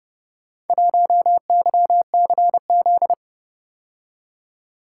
Morse code, heard as a keyed tone, about 30 wpm, 700 Hz. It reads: 1YCZ